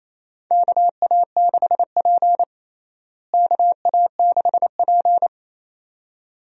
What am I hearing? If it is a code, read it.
Morse code, heard as a keyed tone, about 28 wpm, 700 Hz. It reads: KA6P KA6P
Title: USA Call Signs - 4 Characters - Repeated 2x - 28wpm